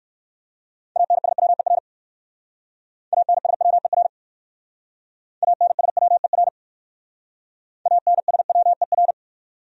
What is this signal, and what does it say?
Morse code, heard as a keyed tone, about 57 wpm, 700 Hz. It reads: ANSWER ANSWER ANSWER ANSWER